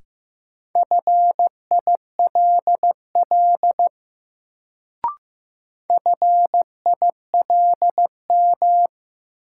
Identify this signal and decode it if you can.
Morse code, heard as a keyed tone, about 15 wpm, 700 Hz. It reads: FILL FILM
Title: Sentences from Top 1000 Words - Review Part 3 - New Word Drill - 15wpm